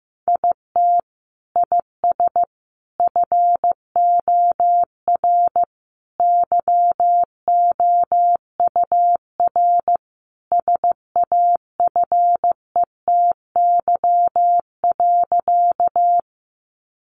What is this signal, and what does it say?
Morse code, heard as a keyed tone, about 15 wpm, 700 Hz. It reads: IT IS FOR YOUR SAFETY.
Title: Sentences from Top 700 Words 15wpm